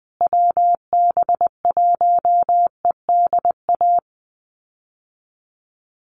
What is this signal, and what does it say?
Morse code, heard as a keyed tone, about 20 wpm, 700 Hz. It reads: WB1EDA
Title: USA Call Signs Repeated 2x 20wpm